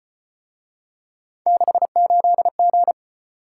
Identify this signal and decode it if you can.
Morse code, heard as a keyed tone, about 33 wpm, 700 Hz. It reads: 68G